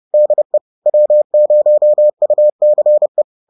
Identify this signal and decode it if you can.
Morse code, heard as a keyed tone, about 30 wpm, 600 Hz. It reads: DE W0UCE
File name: 30wpm